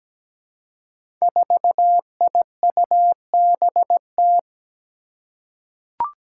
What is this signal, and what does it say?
Morse code, heard as a keyed tone, about 17 wpm, 700 Hz. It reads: 4IUBT
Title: Five Challenging Letter-Number Combinations 17wpm